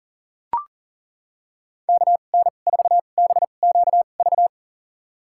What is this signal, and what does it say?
Morse code, heard as a keyed tone, about 40 wpm, spaced as for 29 wpm, 700 Hz. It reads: KN4BQV
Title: USA Call Signs - 2x Character Spacing - 40wpm